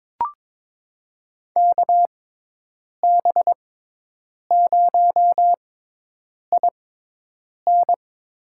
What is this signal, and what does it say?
Morse code, heard as a keyed tone, about 22 wpm, spaced as for 8 wpm, 700 Hz. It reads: KB0IN
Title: USA Call Signs - 6 Characters - 6x Character Spacing - 22wpm